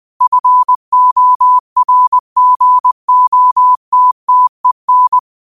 Morse code, heard as a keyed tone, about 20 wpm, 1000 Hz. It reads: FORGOTTEN
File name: qotdfiles / 20wpm